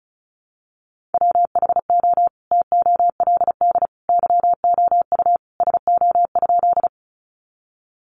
Morse code, heard as a keyed tone, about 34 wpm, 700 Hz. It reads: WHO TOLD YOU SO?
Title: Sentences from Top 400 Words 35wpm